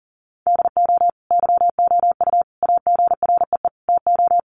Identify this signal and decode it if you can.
Morse code, heard as a keyed tone, about 40 wpm, 700 Hz. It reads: DO YOU AGREE TO
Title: Sentences from Top 1100 Words 40wpm